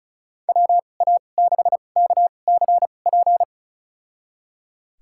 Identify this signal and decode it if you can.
Morse code, heard as a keyed tone, about 36 wpm, spaced as for 25 wpm, 700 Hz. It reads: WA6KCP
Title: USA Call Signs - 6 Characters - 2x Character Spacing - 35wpm